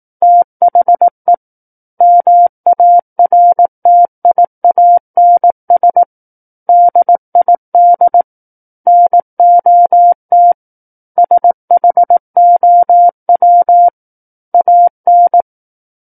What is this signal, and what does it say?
Morse code, heard as a keyed tone, about 18 wpm, 700 Hz. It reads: THE MARTIANS DID NOT SHOW AN
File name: Book One / 09-Chapter 9 - 18 WPM